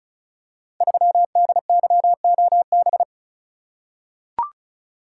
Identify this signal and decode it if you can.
Morse code, heard as a keyed tone, about 36 wpm, 700 Hz. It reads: 3DYOB
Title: Code Groups - 5 Letter:Numbers - 35wpm